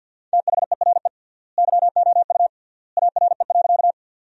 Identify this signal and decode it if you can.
Morse code, heard as a keyed tone, about 50 wpm, 700 Hz. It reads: THERE YOU ARE.
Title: 3x Word-Spacing - Sentences from Top 100 Words - 50wpm